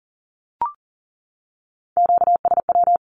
Morse code, heard as a keyed tone, about 40 wpm, 700 Hz. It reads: QSW
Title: Q-Codes 40wpm